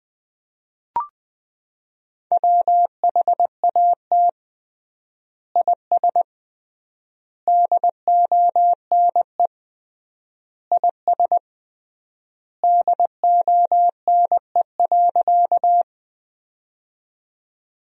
Morse code, heard as a keyed tone, about 20 wpm, 700 Hz. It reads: WHAT IS DONE IS DONE.
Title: 3x Word-Spacing - Sentences from Top 500 Words - 20wpm